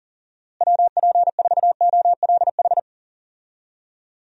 Morse code, heard as a keyed tone, about 40 wpm, 700 Hz. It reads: WP4OLH
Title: USA Call Signs - 6 Characters - Repeated 3x - 40wpm